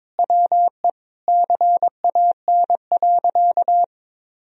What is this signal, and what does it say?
Morse code, heard as a keyed tone, about 22 wpm, 700 Hz. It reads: WE CAN.
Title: Sentences from Top 1000 Words - Review Part 3 - 22wpm